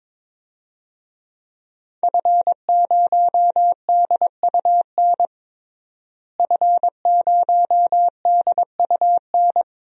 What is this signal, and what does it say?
Morse code, heard as a keyed tone, about 22 wpm, 700 Hz. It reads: F0DUN F0DUN
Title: DXCC-100 Call Signs - 5 Characters - Repeated 2x - 22wpm